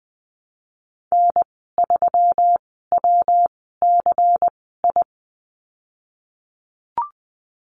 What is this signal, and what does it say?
Morse code, heard as a keyed tone, about 20 wpm, spaced as for 15 wpm, 700 Hz. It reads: N3WCI
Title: 179 Lesson 29g - Slash - Call Signs 20wpm